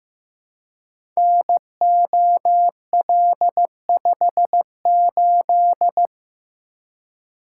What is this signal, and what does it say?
Morse code, heard as a keyed tone, about 15 wpm, 700 Hz. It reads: NOL58
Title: Code Groups - 5 Letter:Numbers - 15wpm